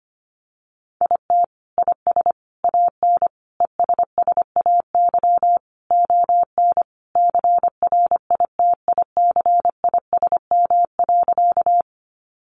Morse code, heard as a keyed tone, about 25 wpm, 700 Hz. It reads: IT IS AN ESSAY ON CRITICISM.
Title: Sentences from Top 1500 Words 25wpm